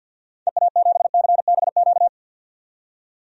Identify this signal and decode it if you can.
Morse code, heard as a keyed tone, about 53 wpm, 700 Hz. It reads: EA7CBQ
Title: DXCC-100 Call Signs - 6 Characters - 50wpm